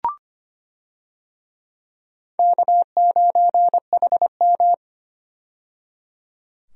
Morse code, heard as a keyed tone, about 25 wpm, 700 Hz. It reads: K9HM
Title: USA Call Signs - 4 Characters - 25wpm